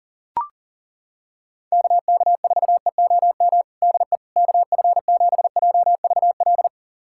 Morse code, heard as a keyed tone, about 40 wpm, 700 Hz. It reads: KK4EOM DE KF7JVL
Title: USA Callsign DE Callsign - 6 Characters - 40wpm